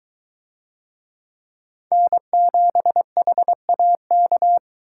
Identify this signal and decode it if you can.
Morse code, heard as a keyed tone, about 22 wpm, 700 Hz. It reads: N7HAK